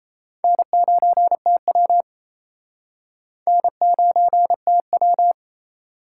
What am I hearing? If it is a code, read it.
Morse code, heard as a keyed tone, about 32 wpm, 700 Hz. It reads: N9TW N9TW
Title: Speed Racing - USA 1x2 Call Signs 28wpm